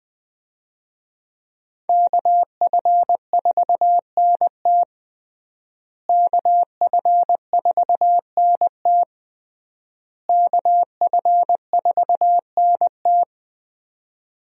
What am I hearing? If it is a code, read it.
Morse code, heard as a keyed tone, about 21 wpm, 700 Hz. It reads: KF4NT KF4NT KF4NT
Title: USA Call Signs - 5 Characters - Repeated 3x - 20wpm